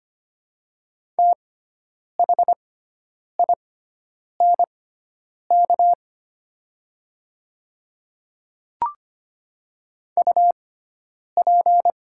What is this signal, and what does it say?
Morse code, heard as a keyed tone, about 25 wpm, spaced as for 9 wpm, 700 Hz. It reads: THINK UP
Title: 6x Character Spacing - Top 200 Words - 25wpm